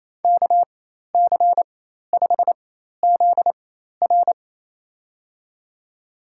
Morse code, heard as a keyed tone, about 28 wpm, spaced as for 13 wpm, 700 Hz. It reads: KC5ZR
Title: USA Call Signs - 5 Characters - 4x Character Spacing - 28wpm